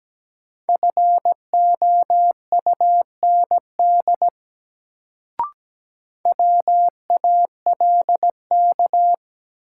Morse code, heard as a keyed tone, about 17 wpm, 700 Hz. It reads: FOUND WALK
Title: Sentences from Top 300 Words - New Word Drill - 17wpm